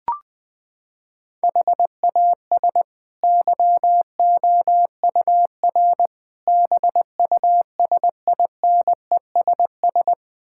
Morse code, heard as a keyed tone, about 20 wpm, 700 Hz. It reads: HAS YOUR BUSINESS
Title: Sentences from Top 1000 Words - Review Part 6 - 20wpm